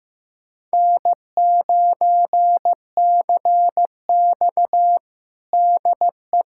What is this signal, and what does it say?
Morse code, heard as a keyed tone, about 15 wpm, 700 Hz. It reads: N9CX DE
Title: USA Callsign DE Callsign - 4 Characters - 15wpm